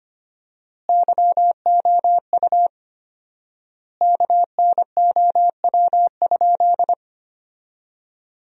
Morse code, heard as a keyed tone, about 25 wpm, 700 Hz. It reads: YOU KNOW?
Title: 4x Word-Spacing - Sentences from Top 100 Words - 25wpm